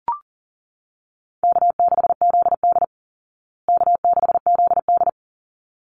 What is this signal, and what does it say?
Morse code, heard as a keyed tone, about 40 wpm, 700 Hz. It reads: K6ZD K6ZD
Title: Easy Call Signs Repeated 2x - 40wpm